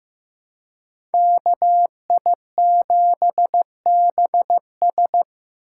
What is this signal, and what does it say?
Morse code, heard as a keyed tone, about 15 wpm, 700 Hz. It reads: KI7BS
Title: USA Call Signs - 5 Characters - 4x Character Spacing - 15wpm